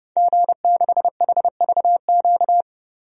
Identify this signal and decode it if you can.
Morse code, heard as a keyed tone, about 30 wpm, 700 Hz. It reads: G6HVQ
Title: DXCC-100 Call Signs - 5 Characters - 30wpm